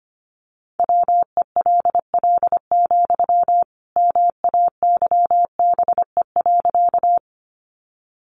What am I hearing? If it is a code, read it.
Morse code, heard as a keyed tone, about 25 wpm, 700 Hz. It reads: WELL, MAYBE.
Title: Sentences from Top 1000 Words 25wpm